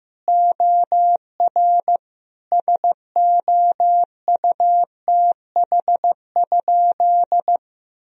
Morse code, heard as a keyed tone, about 15 wpm, 700 Hz. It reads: OR SOUTH?
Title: Sentences from Top 1000 Words - Review Part 2 - 15wpm